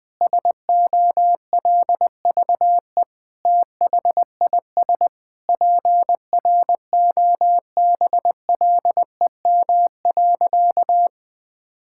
Morse code, heard as a keyed tone, about 20 wpm, 700 Hz. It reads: SOLVE THIS PROBLEM.
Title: Sentences with 7-Word Phrases - 20wpm